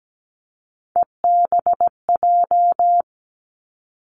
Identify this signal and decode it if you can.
Morse code, heard as a keyed tone, about 17 wpm, 700 Hz. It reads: EBJ
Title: Three Challenging Letter-Number Combinations 17wpm